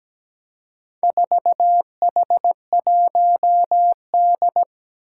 Morse code, heard as a keyed tone, about 17 wpm, 700 Hz. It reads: 4H1D
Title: Four Challenging Letter-Number Combinations 17wpm